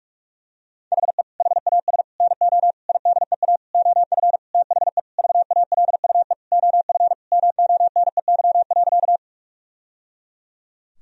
Morse code, heard as a keyed tone, about 46 wpm, 700 Hz. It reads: HE HAS NO IDEA OF THE VALUE OF MONEY.
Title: Sentences from Top 1000 Words - Review Part 3 - 45wpm